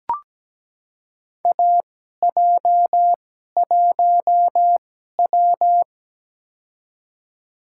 Morse code, heard as a keyed tone, about 17 wpm, spaced as for 12 wpm, 700 Hz. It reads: AJ1W
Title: USA Call Signs - 4 Characters - 2x Character Spacing - 17wpm